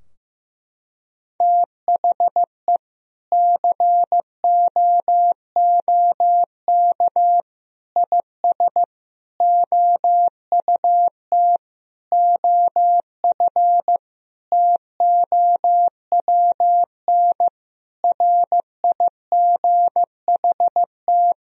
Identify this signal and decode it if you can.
Morse code, heard as a keyed tone, about 15 wpm, 700 Hz. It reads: THE COOK IS OUT OF TOWN RIGHT
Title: Sentences from Top 1000 Words - Review Part 4 - 15wpm